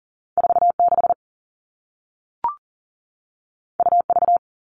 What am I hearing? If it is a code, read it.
Morse code, heard as a keyed tone, about 40 wpm, 700 Hz. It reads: 46 UV
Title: All Cut Numbers - 2-Digit 40wpm